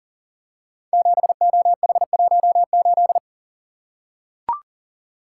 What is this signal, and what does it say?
Morse code, heard as a keyed tone, about 39 wpm, 700 Hz. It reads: 7OH18
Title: Code Groups - 5 Letter:Numbers - 40wpm